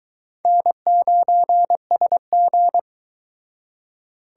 Morse code, heard as a keyed tone, about 22 wpm, 700 Hz. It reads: N9SG